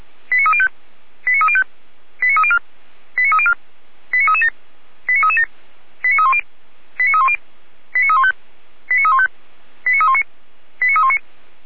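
Five-tone selective call (ZVEI), heard as quick runs of five tones, tones of about 70 ms. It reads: 89305 89305 89304 89304 89337 89337 89310 89310 89315 89315 89319 89319
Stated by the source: ZVEI five-tone selective call received on SDR.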